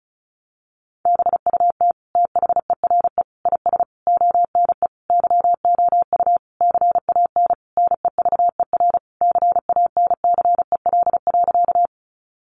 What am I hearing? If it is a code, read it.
Morse code, heard as a keyed tone, about 34 wpm, 700 Hz. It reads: BUT THERE IS ONE YOU CAN NEVER CANCEL.
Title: Sentences from Top 1200 Words 35wpm